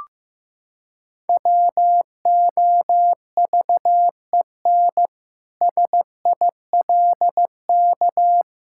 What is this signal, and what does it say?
Morse code, heard as a keyed tone, about 15 wpm, 700 Hz. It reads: WOVEN SILK
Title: Pangram Challenge v2 - 15wpm